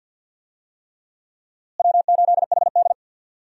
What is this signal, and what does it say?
Morse code, heard as a keyed tone, about 51 wpm, 700 Hz. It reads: W8HD
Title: USA Call Signs - 4 Characters - Repeated 3x - 50wpm